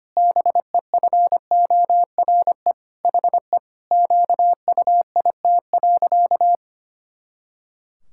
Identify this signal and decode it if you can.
Morse code, heard as a keyed tone, about 25 wpm, 700 Hz. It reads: BEFORE HE QUIT.
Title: Pangram Challenge v2 - 25wpm